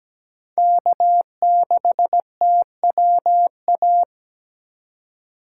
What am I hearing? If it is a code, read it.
Morse code, heard as a keyed tone, about 17 wpm, 700 Hz. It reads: K6TWA
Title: USA Call Signs - 5 Characters - 4x Character Spacing - 17wpm